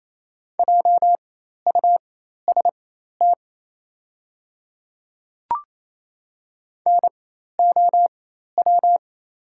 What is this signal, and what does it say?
Morse code, heard as a keyed tone, about 28 wpm, spaced as for 13 wpm, 700 Hz. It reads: JUST NOW
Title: 4x Character Spacing - Top 500 Words - 28wpm